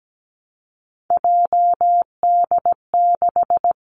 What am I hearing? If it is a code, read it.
Morse code, heard as a keyed tone, about 17 wpm, 700 Hz. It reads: JD6
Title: Three Challenging Letter-Number Combinations 17wpm